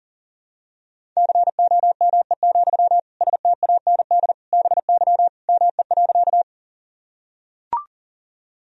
Morse code, heard as a keyed tone, about 40 wpm, 700 Hz. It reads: COME, STAND BY ME.
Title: Sentences from Top 1000 Words - Review Part 1 - 40wpm